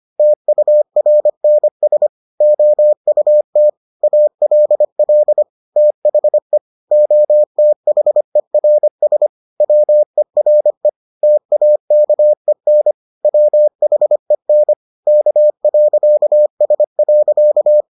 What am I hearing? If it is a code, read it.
Morse code, heard as a keyed tone, about 25 wpm, 600 Hz. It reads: TURNS OUT ALL THE OTHERS WERE TAKEN WHEN K.S.